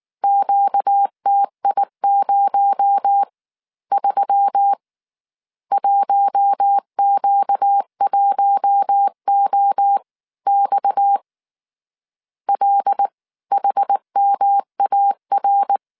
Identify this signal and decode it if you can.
Morse code, heard as a keyed tone, about 19 wpm, spaced as for 16 wpm, 800 Hz. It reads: QTI03 1Q1OX LHMAR
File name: CW / DET / 1 6 1500C S C 18wpm 2356